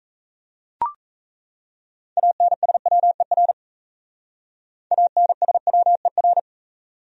Morse code, heard as a keyed tone, about 36 wpm, 700 Hz. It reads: ANSWER ANSWER